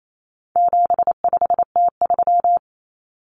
Morse code, heard as a keyed tone, about 28 wpm, 700 Hz. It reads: 75T3
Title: Common Cut Numbers - 4-Digit 28wpm